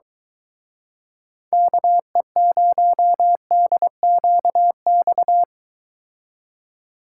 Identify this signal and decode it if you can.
Morse code, heard as a keyed tone, about 24 wpm, 700 Hz. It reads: KE0DQX